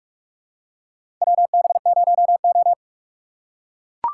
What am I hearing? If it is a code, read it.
Morse code, heard as a keyed tone, about 43 wpm, 700 Hz. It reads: WB0O
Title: USA Call Signs - 4 Characters - 4x Character Spacing - 45wpm